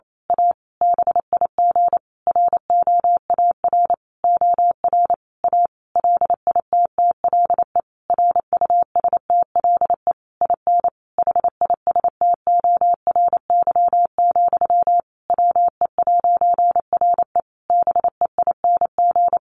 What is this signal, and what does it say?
Morse code, heard as a keyed tone, about 28 wpm, 700 Hz. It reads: A BIG ROAR OR A LITTLE RUSTLE IN HISTORY, WE'RE BEING
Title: A Sound Of Thunder 28wpm